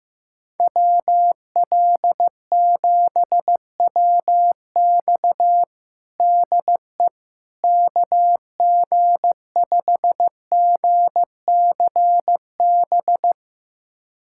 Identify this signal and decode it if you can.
Morse code, heard as a keyed tone, about 15 wpm, 700 Hz. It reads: WL7WX DE KG5GCB
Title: USA Callsign DE Callsign 15wpm